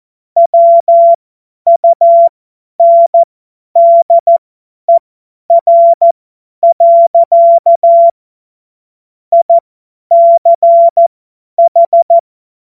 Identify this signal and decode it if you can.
Morse code, heard as a keyed tone, about 14 wpm, spaced as for 10 wpm, 680 Hz. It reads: WUNDER. ICH